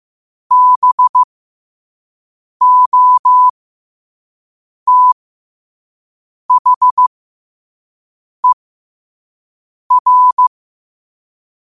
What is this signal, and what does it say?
Morse code, heard as a keyed tone, about 15 wpm, spaced as for 5 wpm, 1000 Hz. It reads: BOTHER
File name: qotdfiles / 5wpm